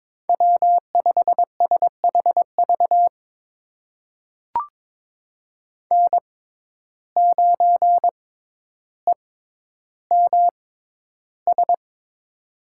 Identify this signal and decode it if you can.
Morse code, heard as a keyed tone, about 22 wpm, spaced as for 11 wpm, 700 Hz. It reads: W5SHV N9EMS
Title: USA Call Signs - 5 Characters - 6x Character Spacing - 22wpm